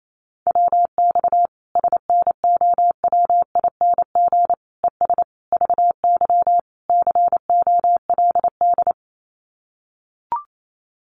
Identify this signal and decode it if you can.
Morse code, heard as a keyed tone, about 28 wpm, 700 Hz. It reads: WX SNOWING ES VY COLD
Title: Common QSO Phrases 28wpm